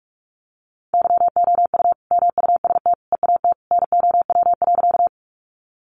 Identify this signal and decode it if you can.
Morse code, heard as a keyed tone, about 44 wpm, 700 Hz. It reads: YOU MUST EAT NOW.
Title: Sentences from Top 300 Words 45wpm